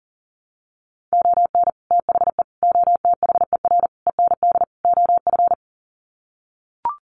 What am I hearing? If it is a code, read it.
Morse code, heard as a keyed tone, about 40 wpm, 700 Hz. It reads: ON THE OTHER END OF
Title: Sets of 5 Words Encore 40wpm